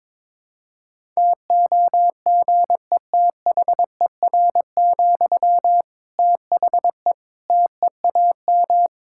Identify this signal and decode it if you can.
Morse code, heard as a keyed tone, about 22 wpm, 700 Hz. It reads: TOGETHER, THE TEAM
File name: Difficult Sentences with Hints - 22wpm